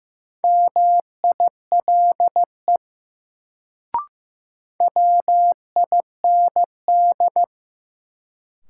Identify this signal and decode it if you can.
Morse code, heard as a keyed tone, about 15 wpm, 700 Hz. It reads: MILE WIND
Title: Sentences from Top 400 Words - New Word Drill - 15wpm